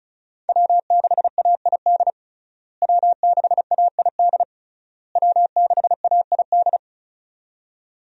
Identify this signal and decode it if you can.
Morse code, heard as a keyed tone, about 36 wpm, 700 Hz. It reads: W6AID W6AID W6AID
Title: USA Call Signs - 5 Characters - Repeated 3x - 35wpm